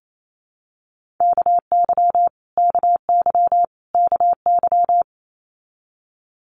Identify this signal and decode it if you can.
Morse code, heard as a keyed tone, about 28 wpm, 700 Hz. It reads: KY KY KY
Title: Multi-letter Phonemes Repeated 3x 28wpm